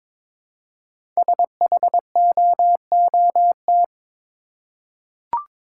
Morse code, heard as a keyed tone, about 22 wpm, 700 Hz. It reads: SHOOT
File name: 5 Letter Words - First 3 Letters Given - 22wpm